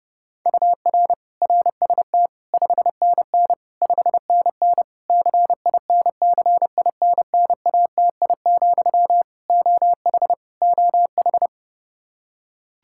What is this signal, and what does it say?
Morse code, heard as a keyed tone, about 30 wpm, 700 Hz. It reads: UR RST 5NN 5NN CINCINNATI, OH OH
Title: Practice QSO - Exchange 1 - Beginning - 30wpm